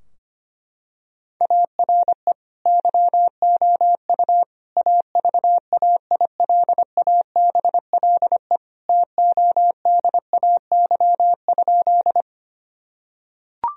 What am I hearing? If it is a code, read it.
Morse code, heard as a keyed tone, about 25 wpm, 700 Hz. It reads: ARE YOU AVAILABLE TODAY?
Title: Sentences from Top 1000 Words - Review Part 3 - 25wpm